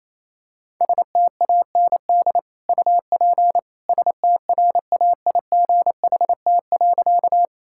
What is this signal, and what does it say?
Morse code, heard as a keyed tone, about 28 wpm, 700 Hz. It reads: STAND UP STRAIGHT.
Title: Sentences from Top 1000 Words - Review Part 6 - 28wpm